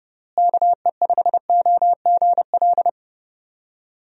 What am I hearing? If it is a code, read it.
Morse code, heard as a keyed tone, about 31 wpm, 700 Hz. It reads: KE5OGL
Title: USA Call Signs - 6 Characters - Repeated 2x - 30wpm